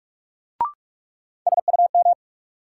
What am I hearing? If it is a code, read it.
Morse code, heard as a keyed tone, about 44 wpm, 700 Hz. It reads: SUM
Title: Sentences from Top 1000 Words - Review Part 6 - New Word Drill - 45wpm